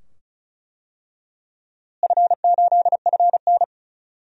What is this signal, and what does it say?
Morse code, heard as a keyed tone, about 34 wpm, 700 Hz. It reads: F8FN
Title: DXCC-100 Call Signs - 4 Characters - Repeated 2x - 35wpm